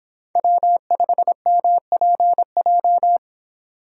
Morse code, heard as a keyed tone, about 26 wpm, 700 Hz. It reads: W5MPJ